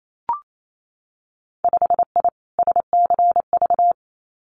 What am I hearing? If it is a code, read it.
Morse code, heard as a keyed tone, about 28 wpm, 700 Hz. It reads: 5I SCV
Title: Winter Field Day Exchanges Encore 28wpm